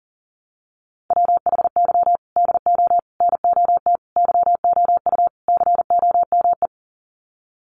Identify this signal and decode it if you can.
Morse code, heard as a keyed tone, about 40 wpm, 700 Hz. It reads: WHY DO NOT YOU COME
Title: Sets of 5 Words Encore2 40wpm